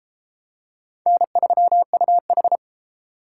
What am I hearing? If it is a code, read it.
Morse code, heard as a keyed tone, about 30 wpm, 700 Hz. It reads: N3UH